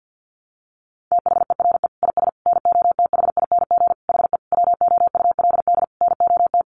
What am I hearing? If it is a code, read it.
Morse code, heard as a keyed tone, about 50 wpm, 700 Hz. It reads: THERE IS NOTHING HE WOULD NOT
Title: Sentences from Top 600 Words 50wpm